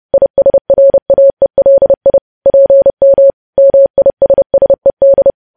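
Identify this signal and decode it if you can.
Morse code, heard as a keyed tone, about 30 wpm, 550 Hz. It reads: ISRAELI PM MISSED
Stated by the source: Morse news headlines sent at 30.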